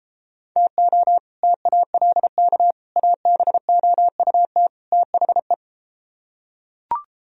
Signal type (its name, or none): Morse code